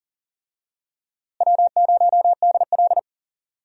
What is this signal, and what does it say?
Morse code, heard as a keyed tone, about 40 wpm, 700 Hz. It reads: W0DL
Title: USA Call Signs - 4 Characters - Repeated 2x - 40wpm